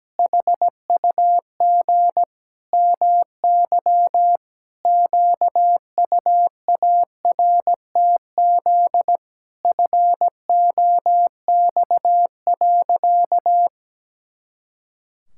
Morse code, heard as a keyed tone, about 17 wpm, 700 Hz. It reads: HUG MY QUARTZ FOX.
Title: Pangram Challenge v2 - 17wpm